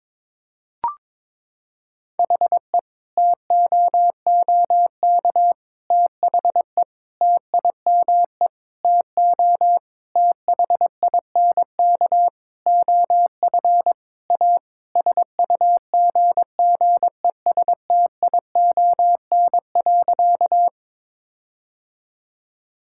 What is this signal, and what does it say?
Morse code, heard as a keyed tone, about 22 wpm, 700 Hz. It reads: HE TOOK THE TIME TO THINK OF A SUGGESTION.
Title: Sentences from Top 1600 Words 22wpm